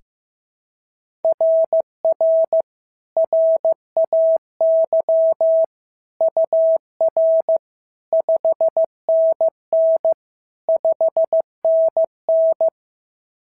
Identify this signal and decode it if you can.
Morse code, heard as a keyed tone, about 14 wpm, 650 Hz. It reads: RR RAY UR 5NN 5NN
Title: SOTA QSO Practice 15wpm